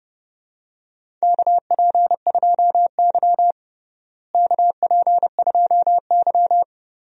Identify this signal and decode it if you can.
Morse code, heard as a keyed tone, about 30 wpm, 700 Hz. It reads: KP2Y KP2Y
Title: DXCC-100 Call Signs - 4 Characters - Repeated 2x - 30wpm